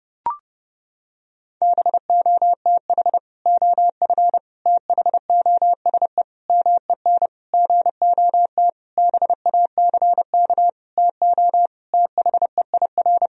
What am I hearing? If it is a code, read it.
Morse code, heard as a keyed tone, about 30 wpm, 700 Hz. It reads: BOTH OF THOSE MEN GOT BACK TO THEIR
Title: Sentences from Top 1000 Words - Review Part 1 - 30wpm